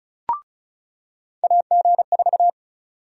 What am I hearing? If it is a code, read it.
Morse code, heard as a keyed tone, about 36 wpm, 700 Hz. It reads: AG4
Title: USA Call Sign Prefixes - 35wpm